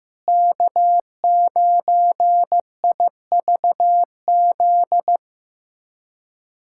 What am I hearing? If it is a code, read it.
Morse code, heard as a keyed tone, about 16 wpm, 700 Hz. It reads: K9IVZ
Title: USA Callsign DE Callsign - 5 Characters - 15wpm